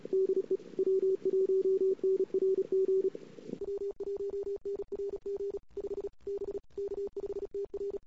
Morse code, heard as a keyed tone, about 36 wpm, 410 Hz. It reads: DE W1NRG W1NRG 5 B K5TR
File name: CW / 2008-ss-cw-k5tr-op-n5rz -21:00